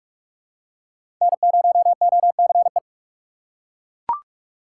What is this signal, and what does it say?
Morse code, heard as a keyed tone, about 41 wpm, 700 Hz. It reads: N0OCE
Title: USA Call Signs - 5 Characters - 4x Character Spacing - 45wpm